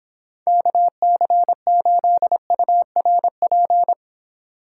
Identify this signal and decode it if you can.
Morse code, heard as a keyed tone, about 26 wpm, 700 Hz. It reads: KC8URP